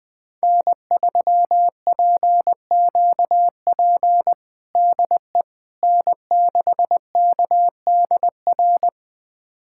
Morse code, heard as a keyed tone, about 21 wpm, 700 Hz. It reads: N3PQP DE N6KDR
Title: USA Callsign DE Callsign - 5 Characters - 20wpm